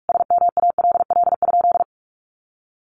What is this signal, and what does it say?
Morse code, heard as a keyed tone, about 44 wpm, 700 Hz. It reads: SMALL?
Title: Sentences from Top 1200 Words 45wpm